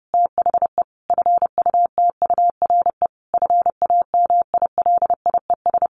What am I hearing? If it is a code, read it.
Morse code, heard as a keyed tone, about 30 wpm, 700 Hz. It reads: THE FUTURE FAMILIES
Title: A Sound Of Thunder 30wpm